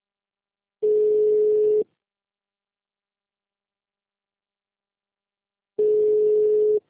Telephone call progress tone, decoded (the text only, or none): ringback tone